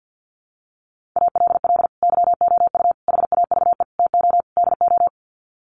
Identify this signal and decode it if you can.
Morse code, heard as a keyed tone, about 50 wpm, 700 Hz. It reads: ALL YOU HAVE TO DO
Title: Sets of 5 Words 50wpm